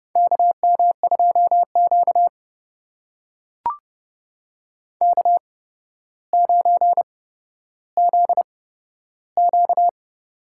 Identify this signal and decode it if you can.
Morse code, heard as a keyed tone, about 31 wpm, spaced as for 13 wpm, 700 Hz. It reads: KM2Q K9ZQ
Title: USA Call Signs - 4 Characters - 8x Character Spacing - 30wpm